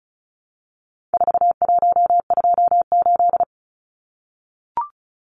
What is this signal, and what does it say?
Morse code, heard as a keyed tone, about 34 wpm, 700 Hz. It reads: 4128
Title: Numbers - 4-Digit 35wpm